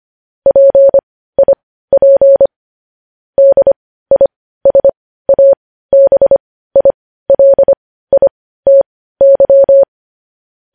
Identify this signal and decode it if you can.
Morse code, heard as a keyed tone, about 25 wpm, spaced as for 15 wpm, 550 Hz. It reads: PIP DISABILITY
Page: news-headlines-in-morse-code-at-15-wpm